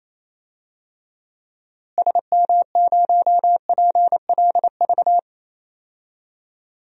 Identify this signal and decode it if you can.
Morse code, heard as a keyed tone, about 28 wpm, 700 Hz. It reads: SM0PLV
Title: DXCC-100 Call Signs - 6 Characters - Repeated 2x - 28wpm